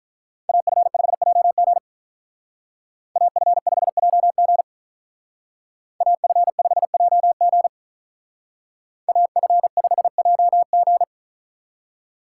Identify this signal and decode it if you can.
Morse code, heard as a keyed tone, about 45 wpm, 700 Hz. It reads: AF5JG AF5JG AF5JG AF5JG